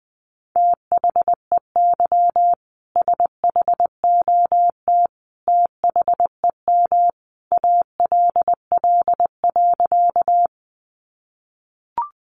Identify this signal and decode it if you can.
Morse code, heard as a keyed tone, about 20 wpm, 700 Hz. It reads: THEY SHOT THEM ALL.
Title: Sentences from Top 1100 Words 20wpm